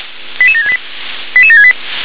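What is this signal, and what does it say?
Five-tone selective call (ZVEI), heard as quick runs of five tones, tones of about 80 ms. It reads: 88768 88768
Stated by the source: ZVEI five-tone call received by SDR.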